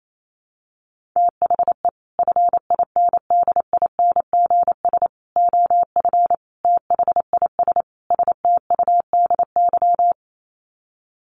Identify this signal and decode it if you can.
Morse code, heard as a keyed tone, about 28 wpm, 700 Hz. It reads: THE FINDINGS OF THIS STUDY
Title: Sets of 5 Words Encore2 28wpm